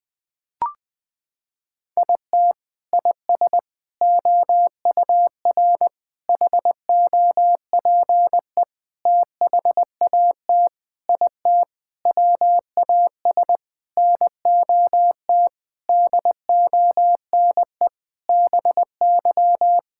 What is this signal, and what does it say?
Morse code, heard as a keyed tone, about 20 wpm, 700 Hz. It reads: IT IS OUR HOPE THAT IT WAS NOT DONE BY
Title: Sentences from Top 1000 Words - Review Part 5 - 20wpm